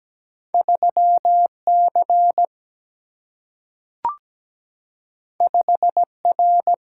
Morse code, heard as a keyed tone, about 17 wpm, 700 Hz. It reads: 3C 5R
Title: Single-Letter Single-Number 17wpm